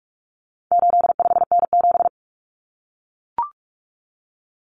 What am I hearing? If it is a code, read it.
Morse code, heard as a keyed tone, about 44 wpm, 700 Hz. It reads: 85N7
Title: Common Cut Numbers - 4-Digit 45wpm